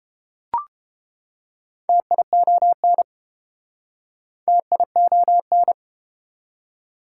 Morse code, heard as a keyed tone, about 34 wpm, 700 Hz. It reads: TION TION